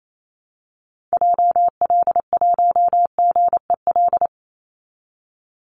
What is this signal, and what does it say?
Morse code, heard as a keyed tone, about 28 wpm, 690 Hz. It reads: JL1GEL
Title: Call Signs Repeated 3x - 28wpm